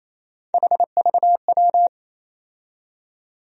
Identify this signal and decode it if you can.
Morse code, heard as a keyed tone, about 28 wpm, 700 Hz. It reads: HVW
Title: USA Call Sign Postfixes - 3 Letters 28wpm